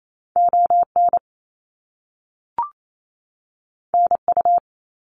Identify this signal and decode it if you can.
Morse code, heard as a keyed tone, about 28 wpm, 700 Hz. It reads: ON NU
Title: Canadian Provinces and Territories 28wpm